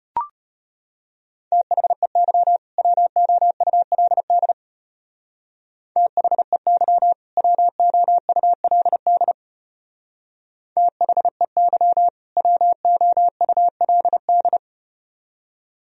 Morse code, heard as a keyed tone, about 34 wpm, 700 Hz. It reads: THEY WOULD THEY WOULD THEY WOULD